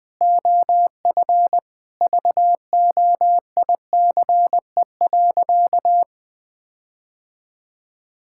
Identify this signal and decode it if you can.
Morse code, heard as a keyed tone, about 20 wpm, 700 Hz. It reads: OF VOICE.
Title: Sentences from Top 1600 Words 20wpm